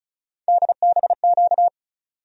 Morse code, heard as a keyed tone, about 34 wpm, 700 Hz. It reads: DBQ
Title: USA Call Sign Postfixes - 3 Letters 35wpm